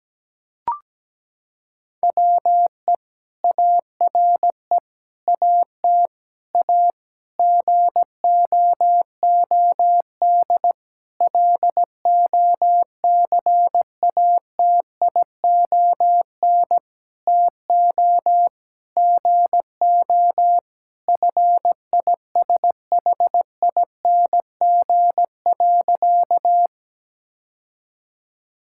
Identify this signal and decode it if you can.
Morse code, heard as a keyed tone, about 17 wpm, 700 Hz. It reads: WE ARE AT A GOOD LOCATION TO GO FISHING.
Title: Sentences from Top 1000 Words - Review Part 6 - 17wpm